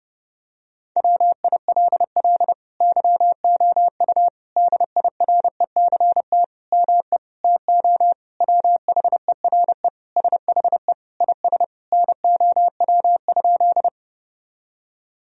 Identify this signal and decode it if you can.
Morse code, heard as a keyed tone, about 30 wpm, 700 Hz. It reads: WILL YOU DIRECT ME TO WHERE SHE IS NOW?
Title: Sentences from Top 1000 Words - Review Part 2 - 30wpm